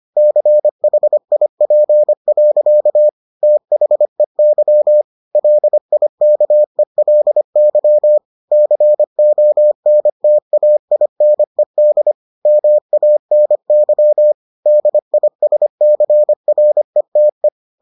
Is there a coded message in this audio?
Yes: Morse code from 0.2 to 17.5 s, heard as a keyed tone, about 25 wpm, 600 Hz. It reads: CHIP. THEY LIKELY CONTAINED MANY DISCRETE